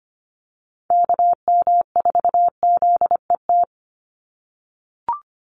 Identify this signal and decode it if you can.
Morse code, heard as a keyed tone, about 25 wpm, 700 Hz. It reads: KM4ZET
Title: USA Call Signs 25wpm